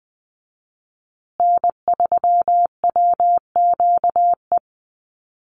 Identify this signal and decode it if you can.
Morse code, heard as a keyed tone, about 20 wpm, 700 Hz. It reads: N3WQE
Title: USA Call Signs Repeated 2x 20wpm